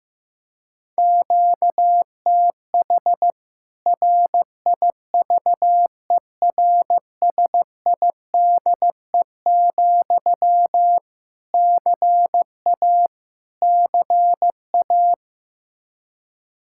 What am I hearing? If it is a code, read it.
Morse code, heard as a keyed tone, about 16 wpm, 700 Hz. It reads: QTH RIVERSIDE, CA CA
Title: K1USN SST QSO Practice 15wpm